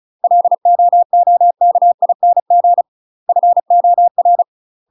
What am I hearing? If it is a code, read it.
Morse code, heard as a keyed tone, about 34 wpm, 700 Hz. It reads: LOOKING FOR
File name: UK MASTER QSOs TRAINING / 35WPM / UK MASTER QSOs 0042